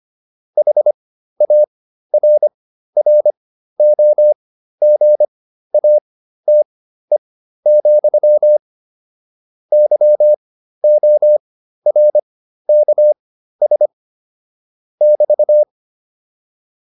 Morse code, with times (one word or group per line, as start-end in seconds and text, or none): 0.6-8.6 HARROGATE,
9.7-13.9 YORKS
15.0-15.6 <BT>